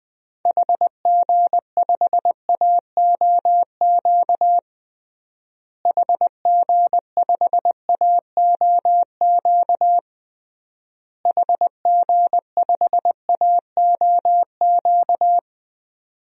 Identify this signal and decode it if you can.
Morse code, heard as a keyed tone, about 20 wpm, 700 Hz. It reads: HG5AOQ HG5AOQ HG5AOQ
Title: DXCC-100 Call Signs - 6 Characters - Repeated 3x - 20wpm